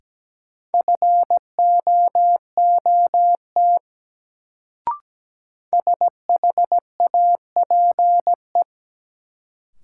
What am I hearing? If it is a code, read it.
Morse code, heard as a keyed tone, about 17 wpm, 700 Hz. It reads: FOOT SHAPE
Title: Sentences from Top 500 Words - New Word Drill - 17wpm